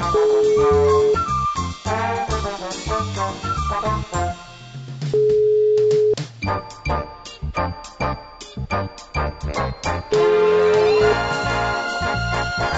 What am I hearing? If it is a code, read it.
Ringback tone (Europe, 425 Hz), with noise.